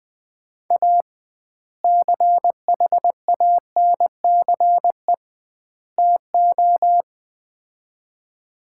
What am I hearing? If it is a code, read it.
Morse code, heard as a keyed tone, about 20 wpm, 700 Hz. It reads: A CHANCE TO
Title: 2x Word-Spacing - Sets of 3 Words - 20wpm